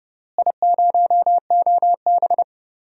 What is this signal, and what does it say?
Morse code, heard as a keyed tone, about 30 wpm, 700 Hz. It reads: I0OB
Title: DXCC-100 Call Signs - 4 Characters - Repeated 3x - 30wpm